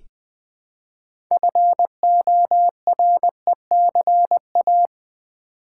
Morse code, heard as a keyed tone, about 20 wpm, 700 Hz. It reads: FORECA
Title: Build A Word - 2 Letters At A Time - 8-Letter Words - 20wpm